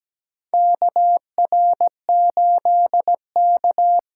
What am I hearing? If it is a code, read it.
Morse code, heard as a keyed tone, about 17 wpm, 700 Hz. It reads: KR8K
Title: USA Call Signs - 4 Characters - 2x Character Spacing - 17wpm